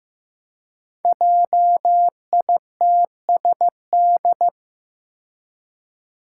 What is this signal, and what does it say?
Morse code, heard as a keyed tone, about 15 wpm, 700 Hz. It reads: JITSD
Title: Five Challenging Letter-Number Combinations 15wpm